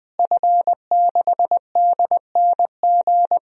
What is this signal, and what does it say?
Morse code, heard as a keyed tone, about 21 wpm, 700 Hz. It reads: F6DNG
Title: DXCC-100 Call Signs - 5 Characters - Repeated 3x - 20wpm